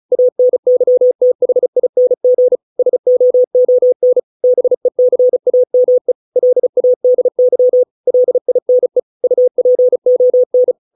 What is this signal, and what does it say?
Morse code, heard as a keyed tone, about 35 wpm, 500 Hz. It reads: ANYTHING SOON BECAME LADY LINE UPON